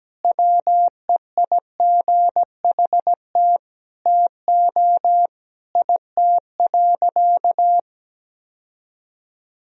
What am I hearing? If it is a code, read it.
Morse code, heard as a keyed tone, about 17 wpm, 700 Hz. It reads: WEIGHT TO IT.
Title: Sentences from Top 1000 Words - Review Part 3 - 17wpm